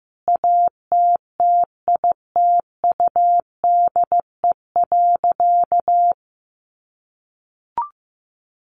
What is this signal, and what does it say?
Morse code, heard as a keyed tone, about 15 wpm, 700 Hz. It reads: ATTITUDE.
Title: Sentences from Top 1300 Words 15wpm